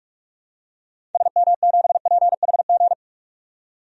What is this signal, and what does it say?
Morse code, heard as a keyed tone, about 42 wpm, 700 Hz. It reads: SM7PHG
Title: DXCC-100 Call Signs - 6 Characters - 45wpm